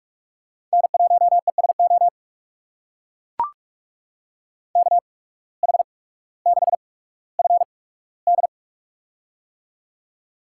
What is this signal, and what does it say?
Morse code, heard as a keyed tone, about 44 wpm, spaced as for 19 wpm, 700 Hz. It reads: N1ESO KH6FD
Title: USA Call Signs - 5 Characters - 8x Character Spacing - 45wpm